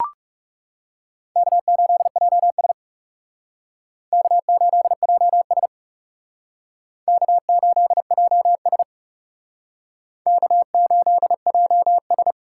Morse code, heard as a keyed tone, about 38 wpm, 700 Hz. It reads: K8JS K8JS K8JS K8JS